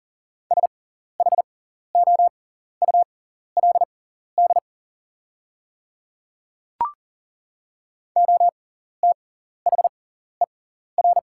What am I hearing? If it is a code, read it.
Morse code, heard as a keyed tone, about 37 wpm, spaced as for 14 wpm, 700 Hz. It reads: SHOULD OTHER
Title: 6x Character Spacing - Top 300 Words - 40wpm